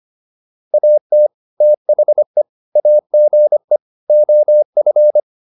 Morse code, heard as a keyed tone, about 25 wpm, 600 Hz. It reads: AT THE AGE OF